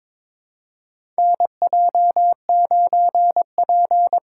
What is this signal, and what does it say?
Morse code, heard as a keyed tone, about 22 wpm, 700 Hz. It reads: NJ9P
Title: USA Call Signs - 6x Character Spacing - 22wpm